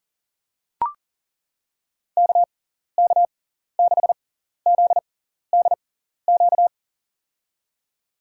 Morse code, heard as a keyed tone, about 40 wpm, spaced as for 14 wpm, 700 Hz. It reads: KK6ZDQ
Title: USA Call Signs - 6 Characters - 6x Character Spacing - 40wpm